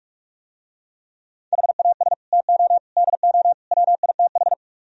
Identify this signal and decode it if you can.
Morse code, heard as a keyed tone, about 44 wpm, 700 Hz. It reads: HAS TO DO WITH